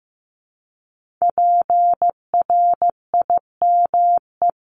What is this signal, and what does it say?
Morse code, heard as a keyed tone, about 15 wpm, 700 Hz. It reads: PRIME
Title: Multi-letter Phonemes Repeated 3x 15wpm